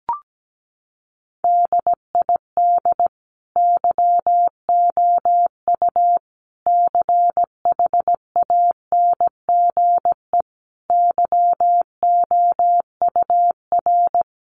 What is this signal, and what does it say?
Morse code, heard as a keyed tone, about 17 wpm, 700 Hz. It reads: DID YOU CHANGE YOUR
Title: Sentences from Top 1400 Words 17wpm